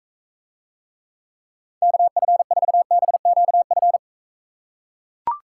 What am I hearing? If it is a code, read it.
Morse code, heard as a keyed tone, about 43 wpm, 700 Hz. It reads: KF4BQF